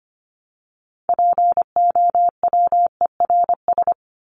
Morse code, heard as a keyed tone, about 25 wpm, 700 Hz. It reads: POWERS
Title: ES:S Suffix - Top 100 Words with Base Words 25wpm